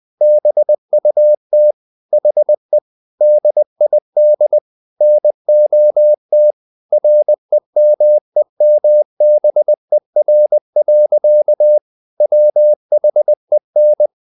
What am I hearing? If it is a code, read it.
Morse code, heard as a keyed tone, about 20 wpm, 600 Hz. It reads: BUT HE DID NOT REMEMBER. WHEN
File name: the old man and the sea / the old man and the sea part 1 20wpm